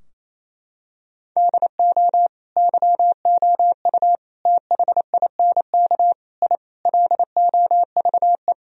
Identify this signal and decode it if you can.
Morse code, heard as a keyed tone, about 28 wpm, 700 Hz. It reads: DO YOU THINK I LOVE
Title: Sentences from Top 1000 Words - Review Part 2 - 28wpm